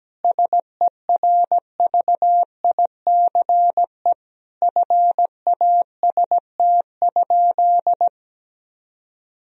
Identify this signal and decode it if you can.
Morse code, heard as a keyed tone, about 17 wpm, 700 Hz. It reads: SERVICE FAST?
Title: Sentences from Top 1000 Words - Review Part 4 - 17wpm